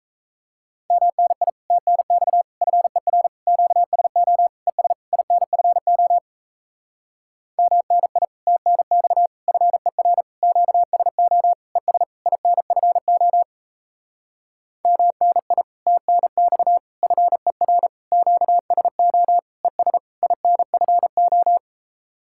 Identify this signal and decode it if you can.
Morse code, heard as a keyed tone, about 37 wpm, 700 Hz. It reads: MNI TNX FER QSO ES INFO MNI TNX FER QSO ES INFO MNI TNX FER QSO ES INFO